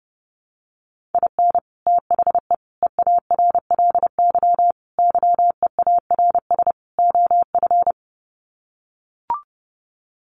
Morse code, heard as a keyed tone, about 30 wpm, 700 Hz. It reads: IN THE EARLY YEARS OF
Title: Sets of 5 Words Encore 30wpm